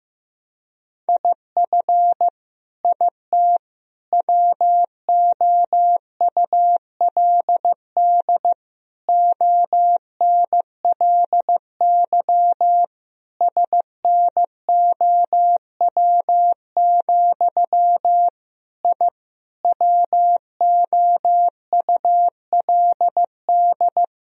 Morse code, heard as a keyed tone, about 15 wpm, 700 Hz. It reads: IF IT WOULD ONLY SNOW, I WOULD
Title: Sentences from Top 1000 Words - Review Part 3 - 15wpm